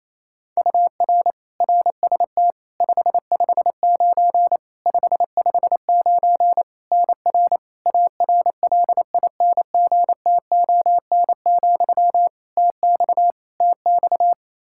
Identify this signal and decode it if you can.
Morse code, heard as a keyed tone, about 28 wpm, 700 Hz. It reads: UR RST 559 559 NR ARLINGTON, TX TX
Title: Practice QSO - Exchange 1 - Beginning - 28wpm